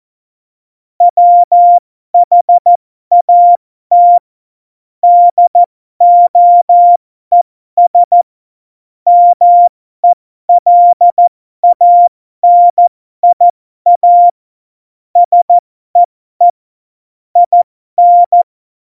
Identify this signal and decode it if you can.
Morse code, heard as a keyed tone, about 14 wpm, spaced as for 12 wpm, 700 Hz. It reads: WHAT DOES MELANIA SEE IN
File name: Morsekurs / Witze